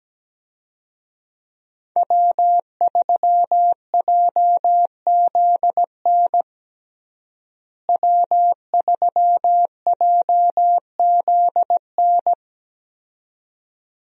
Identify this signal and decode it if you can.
Morse code, heard as a keyed tone, about 17 wpm, 700 Hz. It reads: W3JZN W3JZN
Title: USA Call Signs - 5 Characters - Repeated 2x - 17wpm